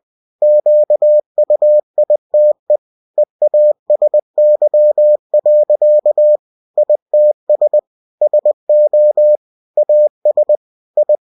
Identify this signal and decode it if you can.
Morse code, heard as a keyed tone, about 20 wpm, 600 Hz. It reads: QUITE EASY. ITS SO AS I
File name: cw-books / winnie-the-pooh / winnie-20wpm-page-0054